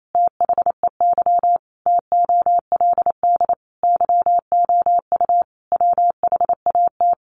Morse code, heard as a keyed tone, about 28 wpm, 700 Hz. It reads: THEY TOLD YOU WHAT
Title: Sentences from Top 1000 Words 28wpm